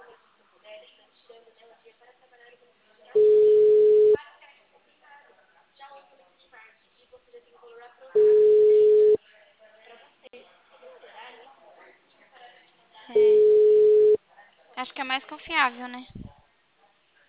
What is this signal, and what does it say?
Ringback tone (Europe, 425 Hz), with speech.